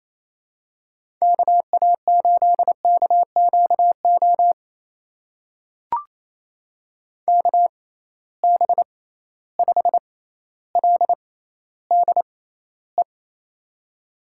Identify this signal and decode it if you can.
Morse code, heard as a keyed tone, about 28 wpm, spaced as for 14 wpm, 700 Hz. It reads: KA8KQO KB5LDE
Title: USA Call Signs - 6 Characters - 6x Character Spacing - 28wpm